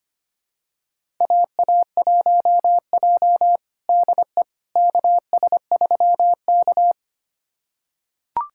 Morse code, heard as a keyed tone, about 25 wpm, 700 Hz. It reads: AA1J DE KS3K
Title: USA Callsign DE Callsign - 4 Characters - 25wpm